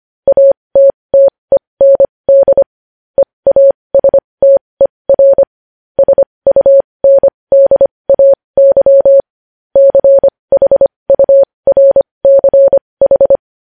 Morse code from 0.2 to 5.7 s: ATTEND EASTER

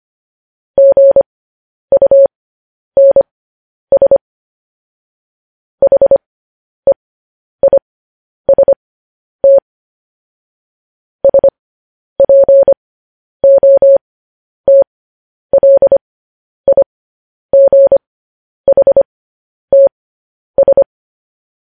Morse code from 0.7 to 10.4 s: GUNS HEIST

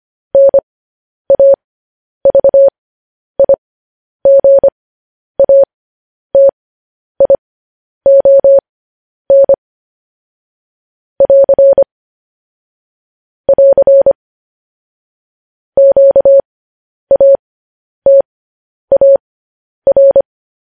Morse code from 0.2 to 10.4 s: NAVIGATION

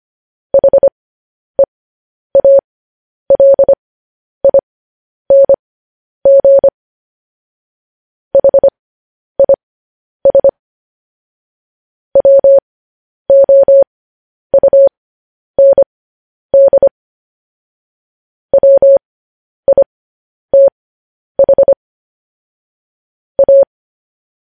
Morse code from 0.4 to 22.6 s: HEALING HIS WOUND WITH

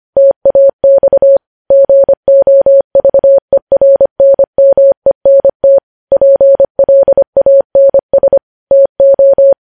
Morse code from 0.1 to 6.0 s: TAX GOVERNMENT

